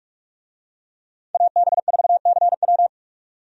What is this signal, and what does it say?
Morse code, heard as a keyed tone, about 42 wpm, 700 Hz. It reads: AB4CW
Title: USA Call Signs - 5 Characters - 45wpm